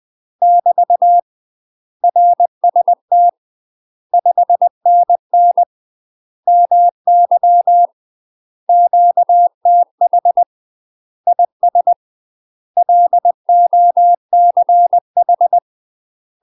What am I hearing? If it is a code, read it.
Morse code, heard as a keyed tone, about 20 wpm, 700 Hz. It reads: <BT> RST 5NN MY QTH IS LOCH
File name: UK MASTER QSOs TRAINING / 20WPM / UK MASTER QSOs 0062